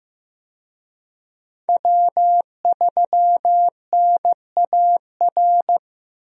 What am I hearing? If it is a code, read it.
Morse code, heard as a keyed tone, about 15 wpm, 700 Hz. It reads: W3NAR
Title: USA Call Signs - 5 Characters - Repeated 3x - 15wpm